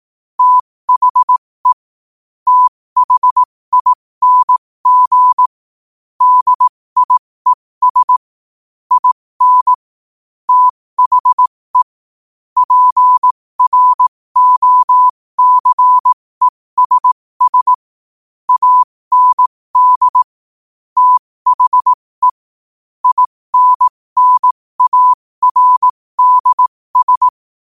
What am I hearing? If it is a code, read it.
Morse code, heard as a keyed tone, about 18 wpm, spaced as for 15 wpm, 1000 Hz. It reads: THE THING DIES IN THE PROCESS AND THE INNARDS
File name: qotdfiles / 15wpm